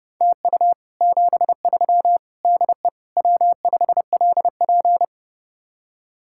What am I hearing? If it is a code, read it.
Morse code, heard as a keyed tone, about 40 wpm, 700 Hz. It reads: TU 73 DE W5LP